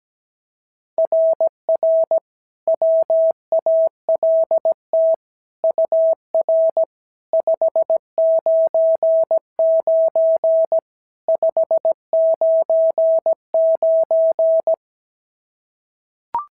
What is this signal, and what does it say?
Morse code, heard as a keyed tone, about 16 wpm, 650 Hz. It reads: RR WALT UR 599 599
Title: SOTA QSO Practice 17wpm